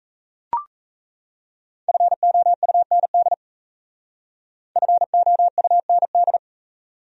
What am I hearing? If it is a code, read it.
Morse code, heard as a keyed tone, about 34 wpm, 700 Hz. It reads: FOUND FOUND
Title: Speed-Racing Top 500 Words 28wpm